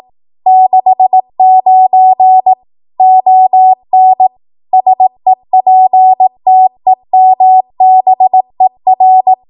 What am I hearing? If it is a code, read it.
Morse code, heard as a keyed tone, about 18 wpm, 750 Hz. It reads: 69 ON SEPTEMBER
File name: Archive / 18 WPM / 18WPM